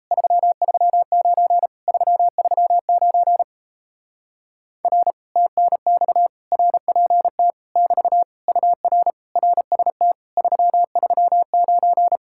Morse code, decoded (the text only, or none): 339 339 R TNX RPT <BT> UR RST 339